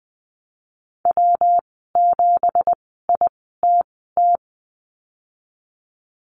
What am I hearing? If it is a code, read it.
Morse code, heard as a keyed tone, about 20 wpm, spaced as for 15 wpm, 700 Hz. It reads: W7ITT
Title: 167 Lesson 27g - Number 7 - Call Signs 20wpm